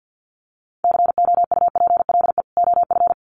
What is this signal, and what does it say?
Morse code, heard as a keyed tone, about 50 wpm, 700 Hz. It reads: COUPLE OF
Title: Sets of 2 Words 50wpm